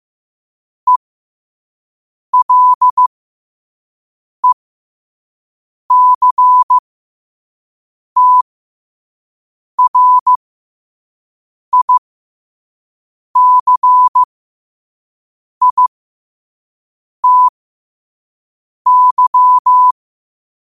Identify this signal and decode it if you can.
Morse code, heard as a keyed tone, about 15 wpm, spaced as for 5 wpm, 1000 Hz. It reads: ELECTRICITY
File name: qotdfiles / 5wpm